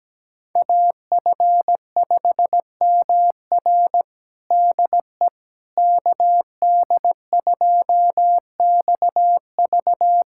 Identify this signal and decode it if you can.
Morse code, heard as a keyed tone, about 17 wpm, 700 Hz. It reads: AF5MR DE KD2XV
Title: USA Callsign DE Callsign - 5 Characters - 17wpm